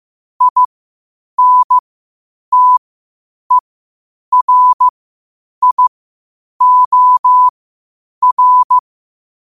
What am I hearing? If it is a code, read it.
Morse code, heard as a keyed tone, about 15 wpm, spaced as for 8 wpm, 1000 Hz. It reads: INTERIOR
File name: qotdfiles / 8wpm